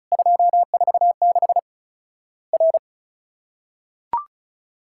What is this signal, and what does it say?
Morse code, heard as a keyed tone, about 34 wpm, 700 Hz. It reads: 246 R
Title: Worked All Europe DX Contest - QTC Exchange - 35wpm